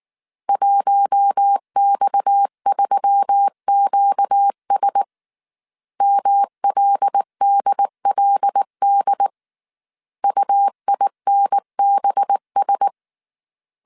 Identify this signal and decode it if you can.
Morse code, heard as a keyed tone, about 19 wpm, 800 Hz. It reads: 1X3QS MLDLD UINBS
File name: CW / DET / 1 6 1500C S A 18wpm 2255